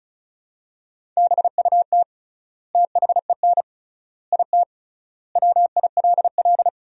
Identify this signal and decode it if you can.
Morse code, heard as a keyed tone, about 36 wpm, 700 Hz. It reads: BUT THEN IT WILL
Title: 3x Word-Spacing - Sentences from Top 300 Words - 35wpm